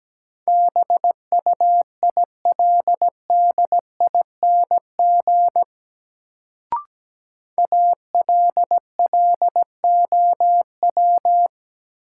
Morse code, heard as a keyed tone, about 17 wpm, 700 Hz. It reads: BUILDING ALLOW
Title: Sentences from Top 700 Words - New Word Drill - 17wpm